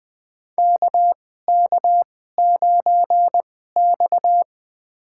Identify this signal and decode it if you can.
Morse code, heard as a keyed tone, about 20 wpm, spaced as for 15 wpm, 700 Hz. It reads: KK9X
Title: USA Call Signs - 2x Character Spacing - 20wpm